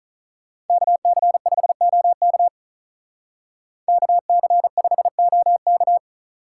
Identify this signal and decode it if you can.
Morse code, heard as a keyed tone, about 45 wpm, 700 Hz. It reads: KC5OK KC5OK